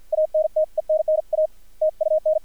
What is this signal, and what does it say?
Morse code, heard as a keyed tone, about 47 wpm, 610 Hz, with radio noise. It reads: ANTENNA TUN
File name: archived / CRSnet